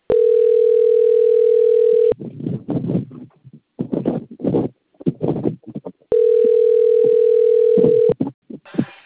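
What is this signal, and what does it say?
Ringback tone (North America), 440 + 480 Hz.
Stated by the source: Outgoing telephone call recorded on a phone system at Cornelio Procopio.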